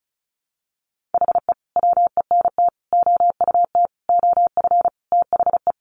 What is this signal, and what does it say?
Morse code, heard as a keyed tone, about 34 wpm, 700 Hz. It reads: HE WENT OUT OF THE
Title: Sentences from Top 1100 Words 35wpm